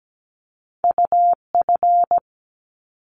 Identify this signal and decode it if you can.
Morse code, heard as a keyed tone, about 17 wpm, 700 Hz. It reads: UF
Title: Top Two Letter Combinations in Words 17wpm